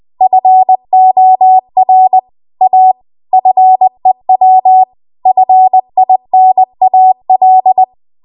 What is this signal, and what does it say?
Morse code, heard as a keyed tone, about 20 wpm, 750 Hz. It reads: FOR A FEW FINAL